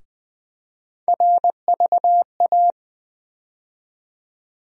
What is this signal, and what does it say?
Morse code, heard as a keyed tone, about 20 wpm, 700 Hz. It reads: RVA
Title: USA Call Sign Postfixes - 3 Letters 20wpm